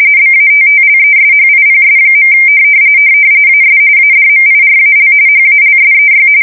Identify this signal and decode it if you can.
RTTY, about 45.45 baud. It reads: BATTLESHIPS. AS ADMIRAL HALSEY
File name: fsk170 / battleshipnj